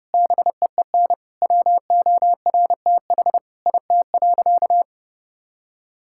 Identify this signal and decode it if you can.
Morse code, heard as a keyed tone, about 30 wpm, 700 Hz. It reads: BEEN WORTH IT.
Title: Sentences from Top 1000 Words - Review Part 5 - 30wpm